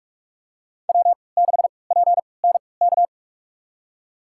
Morse code, heard as a keyed tone, about 47 wpm, spaced as for 26 wpm, 700 Hz. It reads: W6PNK
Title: USA Call Signs - 5 Characters - 3x Character Spacing - 45wpm